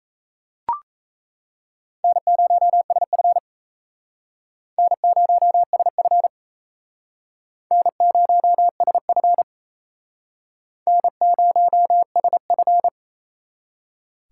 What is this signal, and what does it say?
Morse code, heard as a keyed tone, about 35 wpm, 700 Hz. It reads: N0SF N0SF N0SF N0SF